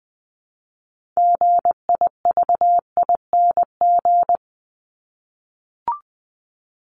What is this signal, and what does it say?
Morse code, heard as a keyed tone, about 20 wpm, 700 Hz. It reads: GIVING
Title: ING Suffix - Top 100 Words with Base Words 20wpm